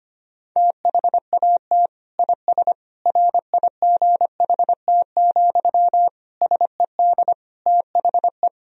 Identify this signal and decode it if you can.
Morse code, heard as a keyed tone, about 25 wpm, 700 Hz. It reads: THAT IS RIGHT, SED THE
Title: Sentences from Top 200 Words and Abbreviations 25wpm